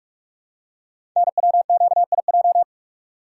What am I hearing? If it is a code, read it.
Morse code, heard as a keyed tone, about 44 wpm, 700 Hz. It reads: NWQIJ
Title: Code Groups - 5 Letter:Numbers - 45wpm